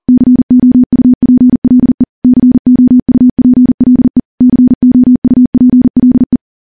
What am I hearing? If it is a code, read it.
Morse code, heard as a keyed tone, about 40 wpm, 250 Hz. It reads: COUPLE COUPLE COUPLE
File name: morse / morse-training-40WPM